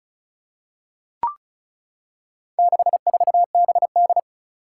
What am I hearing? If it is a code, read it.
Morse code, heard as a keyed tone, about 36 wpm, 700 Hz. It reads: 64BD
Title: Four Challenging Letter-Number Combinations 35wpm